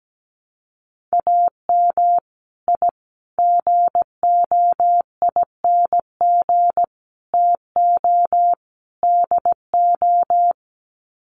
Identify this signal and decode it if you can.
Morse code, heard as a keyed tone, about 17 wpm, 700 Hz. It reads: AM I GOING TO DO
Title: Sets of 5 Words Encore 17wpm